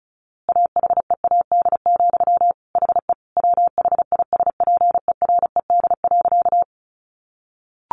Morse code, heard as a keyed tone, about 34 wpm, 700 Hz. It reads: AHEAD, HE WHISPERED.
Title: A Sound Of Thunder 35wpm